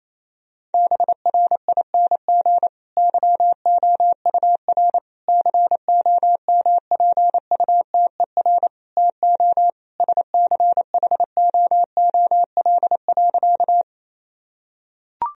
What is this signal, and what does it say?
Morse code, heard as a keyed tone, about 28 wpm, 700 Hz. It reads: BRING YOUR COMPUTER TO SCHOOL.
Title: Sentences from Top 1000 Words - Review Part 3 - 28wpm